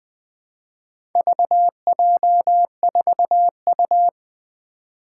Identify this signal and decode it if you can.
Morse code, heard as a keyed tone, about 20 wpm, 700 Hz. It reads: VJ4U
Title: Four Challenging Letter-Number Combinations 20wpm